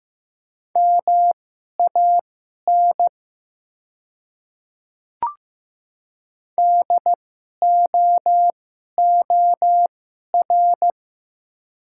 Morse code, heard as a keyed tone, about 15 wpm, spaced as for 11 wpm, 700 Hz. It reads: MAN DOOR
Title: 2x Character Spacing - Top 500 Words - 15wpm